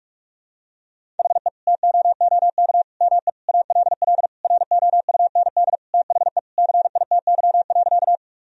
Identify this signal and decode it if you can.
Morse code, heard as a keyed tone, about 46 wpm, 700 Hz. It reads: HE TOOK ME ALL ROUND THE CITY.
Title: Sentences from Top 1000 Words - Review Part 1 - 45wpm